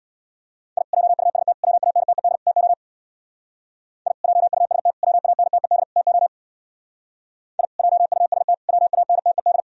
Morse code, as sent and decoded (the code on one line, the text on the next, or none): .. / .--- ..- ... - / .-- .- -. - . -.. / - --- / .. / .--- ..- ... - / .-- .- -. - . -.. / - --- / .. / .--- ..- ... - / .-- .- -. - . -..
I JUST WANTED TO I JUST WANTED TO I JUST WANTED